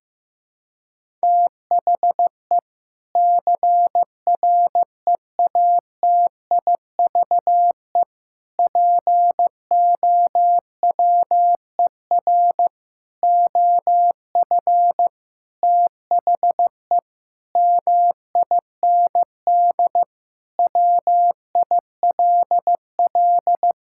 Morse code, heard as a keyed tone, about 15 wpm, 700 Hz. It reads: THE CREATIVE POWER OF THE MIND WILL
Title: Sentences from Top 1000 Words - Review Part 5 - 15wpm